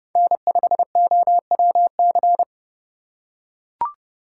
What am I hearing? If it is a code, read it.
Morse code, heard as a keyed tone, about 28 wpm, 700 Hz. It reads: N5OWC